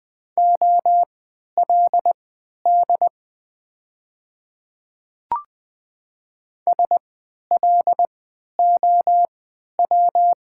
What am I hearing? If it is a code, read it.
Morse code, heard as a keyed tone, about 20 wpm, spaced as for 11 wpm, 700 Hz. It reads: OLD SLOW
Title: 3x Character Spacing - Top 500 Words - 20wpm